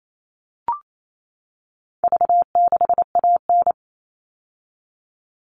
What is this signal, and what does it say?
Morse code, heard as a keyed tone, about 28 wpm, 700 Hz. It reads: V6AN
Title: All Cut Numbers - 4-Digit 28wpm